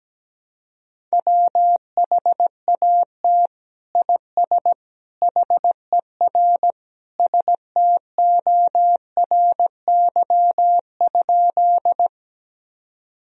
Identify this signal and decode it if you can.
Morse code, heard as a keyed tone, about 17 wpm, 700 Hz. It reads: WHAT IS HER STORY?
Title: Sentences from Top 1000 Words - Review Part 1 - 17wpm